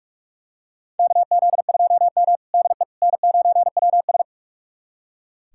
Morse code, heard as a keyed tone, about 46 wpm, 700 Hz. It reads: KZ2M DE N9WS
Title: USA Callsign DE Callsign - 4 Characters - 45wpm